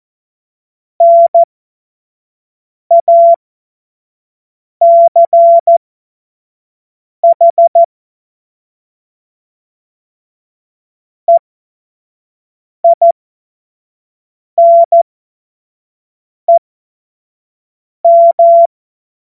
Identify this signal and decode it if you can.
Morse code, heard as a keyed tone, about 14 wpm, spaced as for 5 wpm, 680 Hz. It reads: NACH EINEM
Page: cw-witze